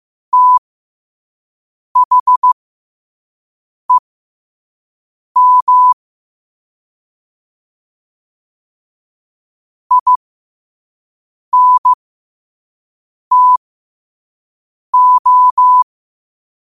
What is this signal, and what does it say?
Morse code, heard as a keyed tone, about 15 wpm, spaced as for 5 wpm, 1000 Hz. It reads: THEM INTO
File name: qotdfiles / 5wpm